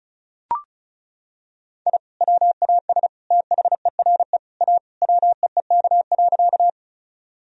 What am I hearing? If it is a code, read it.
Morse code, heard as a keyed tone, about 36 wpm, 700 Hz. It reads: I WAS THERE A WEEK.
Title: Sentences from Top 1000 Words - Review Part 2 - 35wpm